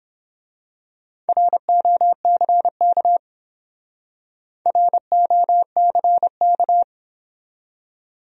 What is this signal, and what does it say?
Morse code, heard as a keyed tone, about 26 wpm, 700 Hz. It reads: ROCK ROCK
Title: Speed-Racing Top 500 Words 22wpm